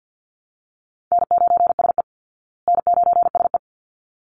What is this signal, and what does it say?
Morse code, heard as a keyed tone, about 50 wpm, 700 Hz. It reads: N9SE N9SE
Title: Easy Call Signs Repeated 2x - 50wpm